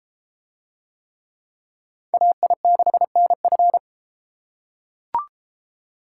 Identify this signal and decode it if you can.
Morse code, heard as a keyed tone, about 30 wpm, 700 Hz. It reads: AI6NF